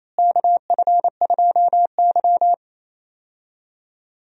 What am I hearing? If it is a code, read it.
Morse code, heard as a keyed tone, about 27 wpm, 700 Hz. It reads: KF2Y
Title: USA Call Signs - 4 Characters - Repeated 2x - 28wpm